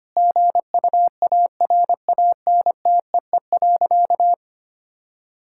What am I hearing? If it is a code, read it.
Morse code, heard as a keyed tone, about 25 wpm, 700 Hz. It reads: GUARANTEE.
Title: Sentences from Top 1600 Words 25wpm